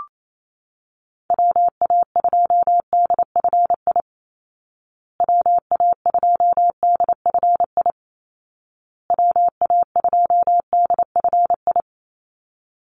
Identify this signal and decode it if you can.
Morse code, heard as a keyed tone, about 28 wpm, 700 Hz. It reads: WA2DFI WA2DFI WA2DFI
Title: Call Signs Repeated 2x - 28wpm